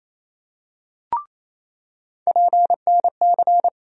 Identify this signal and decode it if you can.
Morse code, heard as a keyed tone, about 28 wpm, 700 Hz. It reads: PNC
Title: USA Call Sign Postfixes - 3 Letters 28wpm